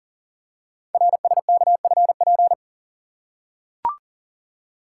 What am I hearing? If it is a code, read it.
Morse code, heard as a keyed tone, about 44 wpm, 700 Hz. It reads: RSKFP
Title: Code Groups - 5 Letters - 40wpm